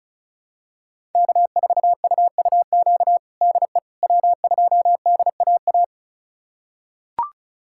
Morse code, heard as a keyed tone, about 36 wpm, 700 Hz. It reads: K4UUQ DE W2DAA
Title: USA Callsign DE Callsign - 5 Characters - 35wpm